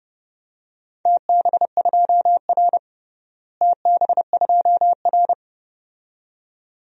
Morse code, heard as a keyed tone, about 30 wpm, 700 Hz. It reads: TB2R TB2R
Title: DXCC-100 Call Signs - 4 Characters - Repeated 2x - 30wpm